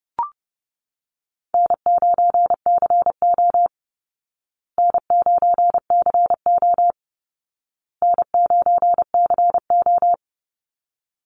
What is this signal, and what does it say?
Morse code, heard as a keyed tone, about 30 wpm, 700 Hz. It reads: N9CO N9CO N9CO
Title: Call Signs Repeated 2x - 30wpm